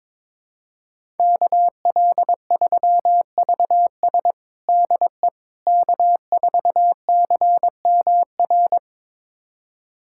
Morse code, heard as a keyed tone, about 22 wpm, 700 Hz. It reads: KL3VS DE K4CMR
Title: USA Callsign DE Callsign - 5 Characters - 22wpm